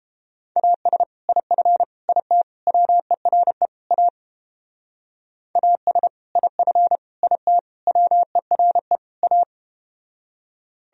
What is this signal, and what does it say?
Morse code, heard as a keyed tone, about 30 wpm, 700 Hz. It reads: AS IF IT WERE A AS IF IT WERE A